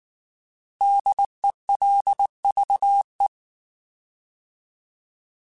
Morse code, heard as a keyed tone, about 19 wpm, 800 Hz. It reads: DELVE
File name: Alphabetical / Alphabetical mp3s / Ds @18